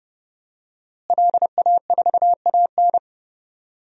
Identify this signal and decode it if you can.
Morse code, heard as a keyed tone, about 29 wpm, 700 Hz. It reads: LA4AN
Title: DXCC-100 Call Signs - 5 Characters - Repeated 2x - 30wpm